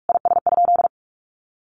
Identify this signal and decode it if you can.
Morse code, heard as a keyed tone, about 44 wpm, 700 Hz. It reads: IS?
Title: Sentences from Top 1500 Words 45wpm